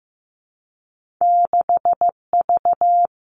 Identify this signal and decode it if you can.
Morse code, heard as a keyed tone, about 15 wpm, 700 Hz. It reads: 6V
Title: Two Challenging Letter-Number Combinations 15wpm